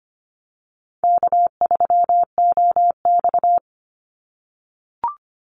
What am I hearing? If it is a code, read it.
Morse code, heard as a keyed tone, about 25 wpm, 700 Hz. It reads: K3OX
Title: Easy Call Signs - 25wpm